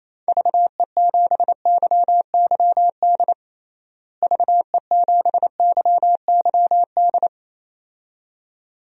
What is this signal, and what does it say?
Morse code, heard as a keyed tone, about 28 wpm, 700 Hz. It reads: VE7YYD VE7YYD
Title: DXCC-100 Call Signs - 6 Characters - Repeated 2x - 28wpm